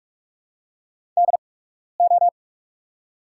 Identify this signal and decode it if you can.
Morse code, heard as a keyed tone, about 44 wpm, spaced as for 12 wpm, 700 Hz. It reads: DO
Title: 8x Character Spacing - Top 200 Words - 45wpm